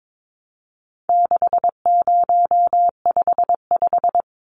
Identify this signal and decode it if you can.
Morse code, heard as a keyed tone, about 22 wpm, 700 Hz. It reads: 6055
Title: Numbers - 4-Digit 22wpm